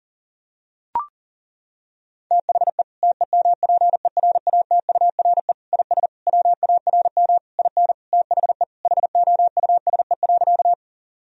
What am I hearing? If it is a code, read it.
Morse code, heard as a keyed tone, about 37 wpm, 700 Hz. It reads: THE TEMPERATURE IS WARM IN THE HOUSE.
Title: Sentences from Top 1000 Words - Review Part 4 - 40wpm